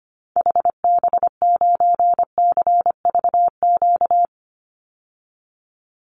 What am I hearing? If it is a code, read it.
Morse code, heard as a keyed tone, about 25 wpm, 700 Hz. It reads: HB9CVQ
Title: Call Signs Repeated 3x - 25wpm